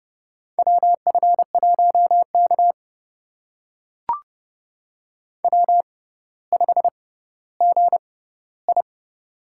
Morse code, heard as a keyed tone, about 30 wpm, spaced as for 15 wpm, 700 Hz. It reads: WF1K W5GI
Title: USA Call Signs - 4 Characters - 6x Character Spacing - 30wpm